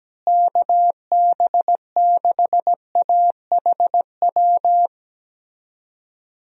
Morse code, heard as a keyed tone, about 17 wpm, 700 Hz. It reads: KB6AHW
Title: USA Call Signs - 6 Characters - Repeated 3x - 17wpm